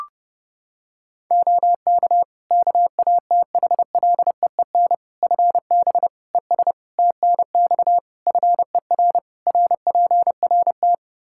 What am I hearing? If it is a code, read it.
Morse code, heard as a keyed tone, about 32 wpm, 700 Hz. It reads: OK KATHLEEN FB ES TNX FER RPRT
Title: Practice QSO - Exchange 2 - Info - 30wpm